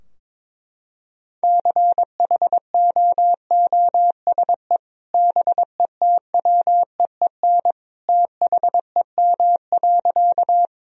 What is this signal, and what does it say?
Morse code, heard as a keyed tone, about 22 wpm, 700 Hz. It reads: CHOOSE BETWEEN THEM.
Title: Sentences from Top 1000 Words - Review Part 4 - 22wpm